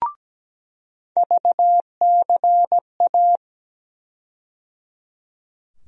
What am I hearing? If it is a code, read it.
Morse code, heard as a keyed tone, about 17 wpm, 700 Hz. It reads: VCA
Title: USA Call Sign Postfixes - 3 Letters 17wpm